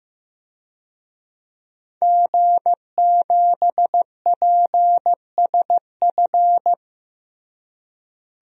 Morse code, heard as a keyed tone, about 16 wpm, 700 Hz. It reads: G7PSF
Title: DXCC-100 Call Signs - 5 Characters - Repeated 3x - 15wpm